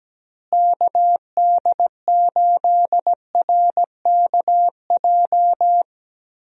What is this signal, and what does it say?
Morse code, heard as a keyed tone, about 17 wpm, 700 Hz. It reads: KD8RKJ
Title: USA Call Signs - 6 Characters - Repeated 3x - 17wpm